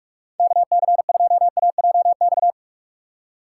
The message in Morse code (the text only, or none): KC2AJX